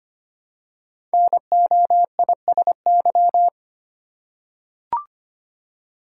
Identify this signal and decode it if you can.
Morse code, heard as a keyed tone, about 25 wpm, 700 Hz. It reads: NOISY
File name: 5 Letter Words - First 3 Letters Given - 25wpm